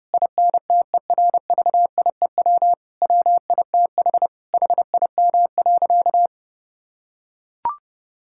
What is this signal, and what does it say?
Morse code, heard as a keyed tone, about 30 wpm, 700 Hz. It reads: INTERVIEW WITH HIM.
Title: Sentences from Top 1600 Words 30wpm